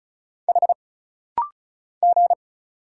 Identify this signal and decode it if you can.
Morse code, heard as a keyed tone, about 36 wpm, 700 Hz. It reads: H G
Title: Single Letter-Number - Rapid-Fire 35wpm